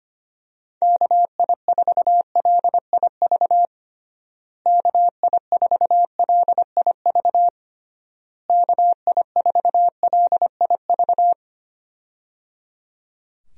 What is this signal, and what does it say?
Morse code, heard as a keyed tone, about 25 wpm, 700 Hz. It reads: KI4LIV KI4LIV KI4LIV
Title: USA Call Signs - 6 Characters - Repeated 3x - 25wpm